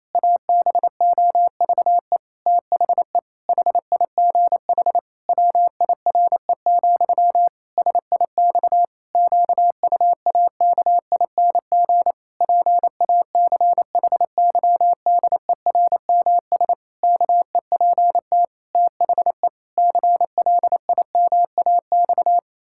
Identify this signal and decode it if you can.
Morse code, heard as a keyed tone, about 28 wpm, 700 Hz. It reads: ABOVE THE HIGH WIRE, SIX QUAKING PACHYDERMS KEPT THE CLIMAX
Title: Pangram Challenge v2 - 28wpm